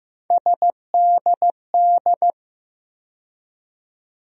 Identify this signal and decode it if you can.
Morse code, heard as a keyed tone, about 15 wpm, 700 Hz. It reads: SDD
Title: USA Call Sign Postfixes - 3 Letters 15wpm